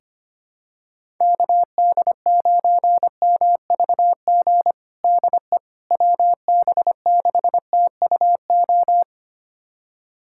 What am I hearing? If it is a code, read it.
Morse code, heard as a keyed tone, about 25 wpm, 700 Hz. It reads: KD9MVG DE WB6TUO
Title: USA Callsign DE Callsign 25wpm